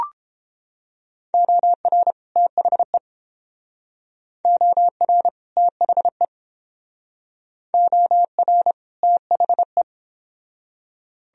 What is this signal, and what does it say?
Morse code, heard as a keyed tone, about 30 wpm, 700 Hz. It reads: OR THE OR THE OR THE